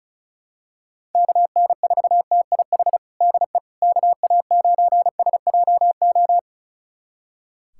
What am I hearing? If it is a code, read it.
Morse code, heard as a keyed tone, about 34 wpm, 700 Hz. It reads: KN4TIH DE KA9SJO
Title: USA Callsign DE Callsign - 6 Characters - 35wpm